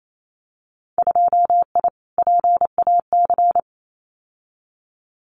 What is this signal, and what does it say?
Morse code, heard as a keyed tone, about 28 wpm, 700 Hz. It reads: 2I PAC
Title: Winter Field Day Exchanges Encore 28wpm